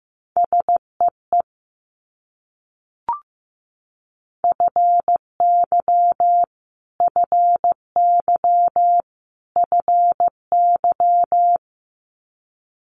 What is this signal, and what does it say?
Morse code, heard as a keyed tone, about 15 wpm, 700 Hz. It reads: SEE FY FY FY
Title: Multi-letter Phonemes Repeated 3x 15wpm